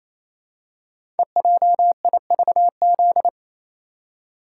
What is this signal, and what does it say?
Morse code, heard as a keyed tone, about 27 wpm, 690 Hz. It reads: EJIVZ
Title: Code Groups - 5 Letters - 28wpm